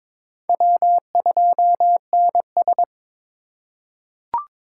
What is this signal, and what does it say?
Morse code, heard as a keyed tone, about 22 wpm, 700 Hz. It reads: W2NS
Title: USA Call Signs - 4 Characters - 3x Character Spacing - 22wpm